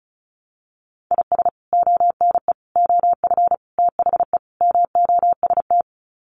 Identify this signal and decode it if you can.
Morse code, heard as a keyed tone, about 34 wpm, 700 Hz. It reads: IS ONE OF THE MOST
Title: Sets of 5 Words 35wpm